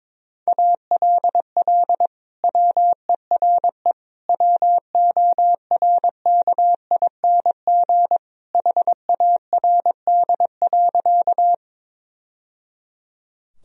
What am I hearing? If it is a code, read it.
Morse code, heard as a keyed tone, about 22 wpm, 700 Hz. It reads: ALL WERE WORKING HARD.
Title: Sentences from Top 1000 Words - Review Part 4 - 22wpm